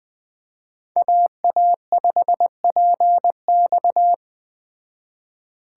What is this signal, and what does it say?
Morse code, heard as a keyed tone, about 20 wpm, 700 Hz. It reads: AA5PX